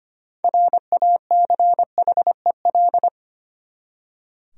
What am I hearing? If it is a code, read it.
Morse code, heard as a keyed tone, about 25 wpm, 700 Hz. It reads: RACHEL
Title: USA Common Names - 25wpm